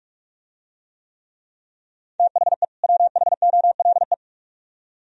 Morse code, heard as a keyed tone, about 49 wpm, 700 Hz. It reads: THE WHOLE